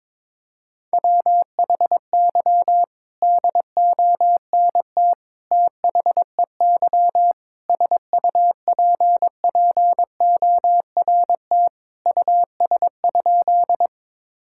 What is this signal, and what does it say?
Morse code, heard as a keyed tone, about 22 wpm, 700 Hz. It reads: WHY DONT THEY SUPPORT US?
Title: Sentences from Top 1000 Words - Review Part 5 - 22wpm